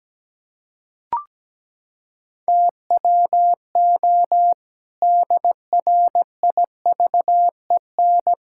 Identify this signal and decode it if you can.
Morse code, heard as a keyed tone, about 17 wpm, 700 Hz. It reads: TWO DRIVEN
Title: Pangram Challenge v2 - 17wpm